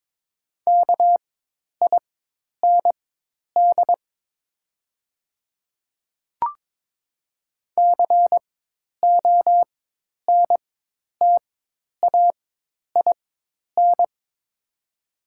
Morse code, heard as a keyed tone, about 22 wpm, spaced as for 10 wpm, 700 Hz. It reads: KIND CONTAIN
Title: 4x Character Spacing - Top 500 Words - 22wpm